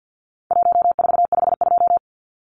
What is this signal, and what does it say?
Morse code, heard as a keyed tone, about 50 wpm, 700 Hz. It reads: 1452
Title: Numbers - Contest Style - 4-Digit 50wpm